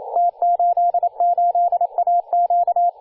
Morse code, heard as a keyed tone, about 28 wpm, 700 Hz, with radio noise. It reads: T88AQ
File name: posts / 2017-jan-06-2301z-t88aq-14-016mhz-cw